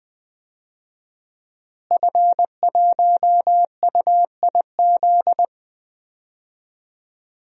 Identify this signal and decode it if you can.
Morse code, heard as a keyed tone, about 20 wpm, 700 Hz. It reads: F1UIZ
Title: DXCC-100 Call Signs - 5 Characters - Repeated 2x - 20wpm